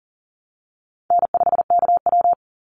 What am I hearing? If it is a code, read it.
Morse code, heard as a keyed tone, about 40 wpm, 700 Hz. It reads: N5KW
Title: Easy Call Signs Repeated 3x - 40wpm